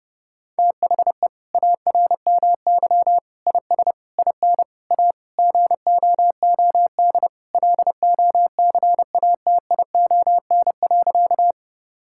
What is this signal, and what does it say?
Morse code, heard as a keyed tone, about 30 wpm, 700 Hz. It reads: THE ARMY IS IN A GOOD LOCATION.
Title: Sentences from Top 1000 Words - Review Part 6 - 30wpm